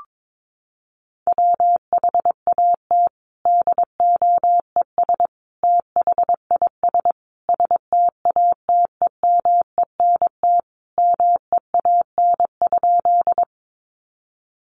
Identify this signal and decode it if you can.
Morse code, heard as a keyed tone, about 22 wpm, 700 Hz. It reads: WHAT DOES THIS STATEMENT MEAN?
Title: Sentences from Top 1200 Words 22wpm